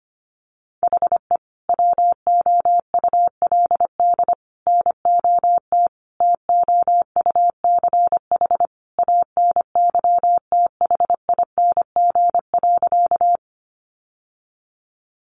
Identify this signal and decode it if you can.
Morse code, heard as a keyed tone, about 25 wpm, 700 Hz. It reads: HE WOULD NOT TOUCH ANYTHING.
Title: Sentences from Top 1200 Words 25wpm